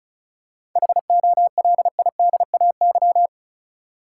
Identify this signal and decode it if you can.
Morse code, heard as a keyed tone, about 36 wpm, 700 Hz. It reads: HOLIDAY
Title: Seven Letter Words - 35wpm